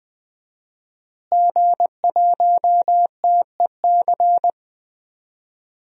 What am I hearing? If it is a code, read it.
Morse code, heard as a keyed tone, about 20 wpm, 700 Hz. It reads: G1TEC
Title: DXCC-100 Call Signs - 5 Characters - Repeated 3x - 20wpm